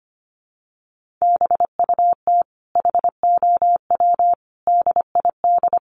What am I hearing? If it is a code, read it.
Morse code, heard as a keyed tone, about 25 wpm, 700 Hz. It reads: BUT HOW DID
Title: Sentences from Top 1000 Words 25wpm